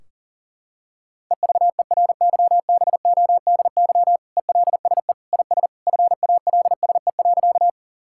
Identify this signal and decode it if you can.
Morse code, heard as a keyed tone, about 37 wpm, 700 Hz. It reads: EVERYBODY ELSE IS FALSE.
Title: Sentences from Top 1600 Words 40wpm